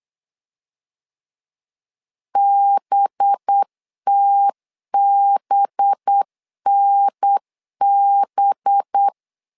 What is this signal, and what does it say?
Morse code, heard as a keyed tone, about 8 wpm, 800 Hz. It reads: BTBNB
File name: LEZIONI / 5 LBVC / 5 LB 8wpm 2816